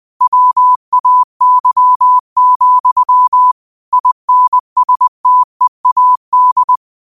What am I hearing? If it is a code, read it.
Morse code, heard as a keyed tone, about 20 wpm, 1000 Hz. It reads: WAY, INSTEAD
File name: qotdfiles / 20wpm